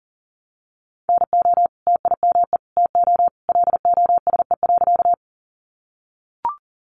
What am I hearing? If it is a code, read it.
Morse code, heard as a keyed tone, about 40 wpm, 700 Hz. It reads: NO TIME TO LOSE.
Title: Sentences from Top 1100 Words 40wpm